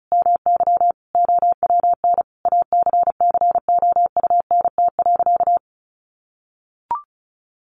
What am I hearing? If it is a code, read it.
Morse code, heard as a keyed tone, about 34 wpm, 700 Hz. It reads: MY OWN ACCOUNT.
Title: Sentences from Top 1400 Words 35wpm